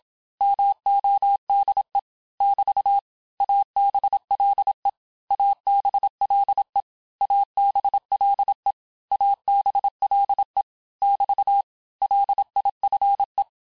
Morse code, heard as a keyed tone, about 27 wpm, 800 Hz. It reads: MODE <BT> ABLE ABLE ABLE ABLE <BT> LIFE
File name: cours-du-mercredi / audios / cours-869-texte-LES-MOTS-vitesse-25-MPM